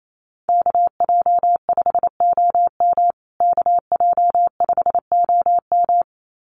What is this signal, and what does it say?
Morse code, heard as a keyed tone, about 28 wpm, 700 Hz. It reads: KJ5OM KJ5OM
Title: USA Call Signs Repeated 2x 28wpm